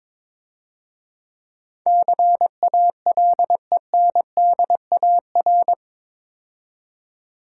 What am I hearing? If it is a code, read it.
Morse code, heard as a keyed tone, about 22 wpm, 700 Hz. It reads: CALENDAR
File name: Easy Words - Speed-Racing with Hints - 22wpm